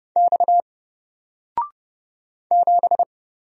Morse code, heard as a keyed tone, about 30 wpm, 700 Hz. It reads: X 7
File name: Single-Letter-Number-30wpm